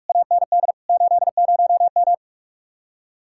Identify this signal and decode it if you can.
Morse code, heard as a keyed tone, about 49 wpm, 700 Hz. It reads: AND 80M